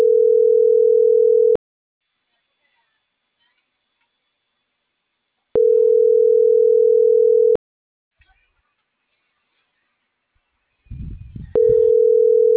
Ringback tone (North America), 440 + 480 Hz.